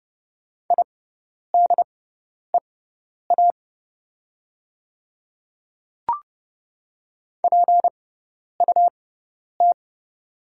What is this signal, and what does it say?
Morse code, heard as a keyed tone, about 29 wpm, spaced as for 10 wpm, 700 Hz. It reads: IDEA PUT
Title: 6x Character Spacing - Top 300 Words - 30wpm